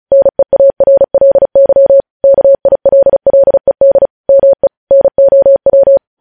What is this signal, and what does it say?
Morse code, heard as a keyed tone, about 36 wpm, 550 Hz. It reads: NEARLY KILLED ME NOW
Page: news headlines in morse code at 35 wpm